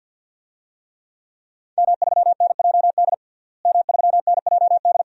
Morse code, heard as a keyed tone, about 51 wpm, 700 Hz. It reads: M3NJD M3NJD
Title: DXCC-100 Call Signs - 5 Characters - Repeated 2x - 50wpm